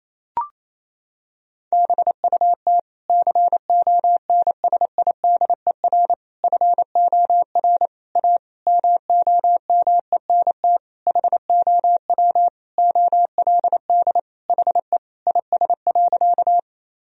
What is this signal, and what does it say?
Morse code, heard as a keyed tone, about 28 wpm, 700 Hz. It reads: BUT CONSIDER FOR A MOMENT HOW OLD HE IS.
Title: Sentences from Top 1000 Words - Review Part 4 - 28wpm